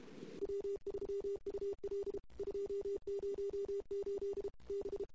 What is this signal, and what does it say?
Morse code, heard as a keyed tone, about 29 wpm, 410 Hz. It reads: W3UL 208 B
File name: CW / 2008-ss-cw-k5tr-op-n5rz -22:00